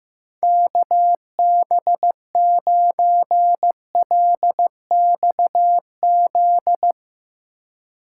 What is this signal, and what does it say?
Morse code, heard as a keyed tone, about 16 wpm, 700 Hz. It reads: KB9LXZ
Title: Speed Racing - USA Call Signs 15wpm